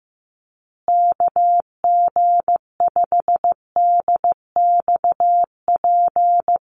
Morse code, heard as a keyed tone, about 15 wpm, 700 Hz. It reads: KG5DXP
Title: USA Call Signs Repeated 3x 15wpm